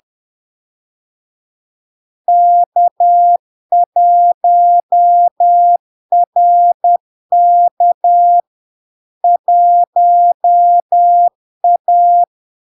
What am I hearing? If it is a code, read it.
Morse code, heard as a keyed tone, about 10 wpm, 700 Hz. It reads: K1RK 1A
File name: LARK-FD-2022-Log-in-CW-10wpm-700-Hz